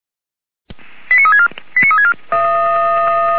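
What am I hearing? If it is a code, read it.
Five-tone selective call (ZVEI), heard as quick runs of five tones, tones of about 70 ms. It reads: 89364 89364
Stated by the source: ZVEI five-tone selective call received with an SDR.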